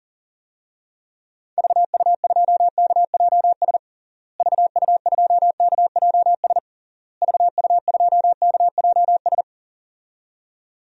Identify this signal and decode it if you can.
Morse code, heard as a keyed tone, about 40 wpm, 700 Hz. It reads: VU2KJS VU2KJS VU2KJS
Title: DXCC-100 Call Signs - 6 Characters - Repeated 3x - 40wpm